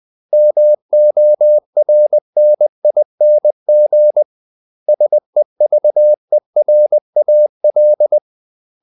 Morse code, heard as a keyed tone, about 20 wpm, 600 Hz. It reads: MORNING SEVERAL
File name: CW / Practice / 500MostCommonEnglishWords20